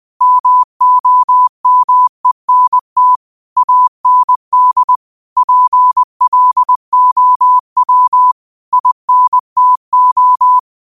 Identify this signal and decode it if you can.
Morse code, heard as a keyed tone, about 20 wpm, 1000 Hz. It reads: MOMENT AND PLOW INTO